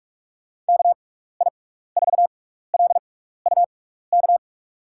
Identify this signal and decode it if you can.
Morse code, heard as a keyed tone, about 42 wpm, spaced as for 15 wpm, 700 Hz. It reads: KI4LUK
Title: USA Call Signs - 6 Characters - 6x Character Spacing - 45wpm